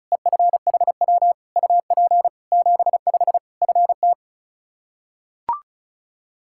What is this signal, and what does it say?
Morse code, heard as a keyed tone, about 36 wpm, 700 Hz. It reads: EFHW UP 75 FT
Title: Practice QSO - Exchange 2 - Info - 35wpm